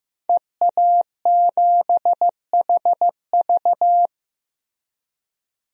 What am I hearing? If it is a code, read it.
Morse code, heard as a keyed tone, about 15 wpm, 700 Hz. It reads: EA7HV
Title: DXCC-100 Call Signs - 5 Characters - Repeated 2x - 15wpm